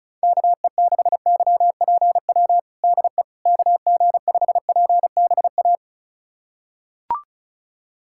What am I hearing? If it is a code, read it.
Morse code, heard as a keyed tone, about 36 wpm, 690 Hz. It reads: KE6YPW DE KG5PBA
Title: USA Callsign DE Callsign - 6 Characters - 35wpm